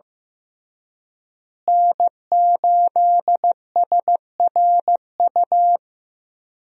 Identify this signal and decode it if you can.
Morse code, heard as a keyed tone, about 15 wpm, 700 Hz. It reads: N8SRU
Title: USA Call Signs - 5 Characters - Repeated 2x - 15wpm